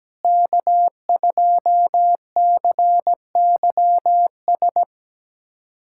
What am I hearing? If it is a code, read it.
Morse code, heard as a keyed tone, about 17 wpm, 700 Hz. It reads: K2CYS
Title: K1USN SST QSO Practice - 17wpm